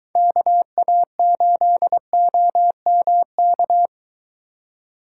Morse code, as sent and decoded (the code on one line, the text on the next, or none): -.- .- ---.. --- -- -.-
KA8OMK